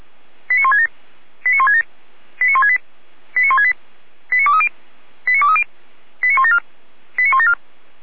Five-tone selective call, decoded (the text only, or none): 89167 89167 89168 89168 89230 89230 89174 89174